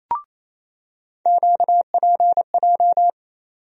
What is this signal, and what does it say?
Morse code, heard as a keyed tone, about 27 wpm, 700 Hz. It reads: QPJ
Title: USA Call Sign Postfixes - 3 Letters 28wpm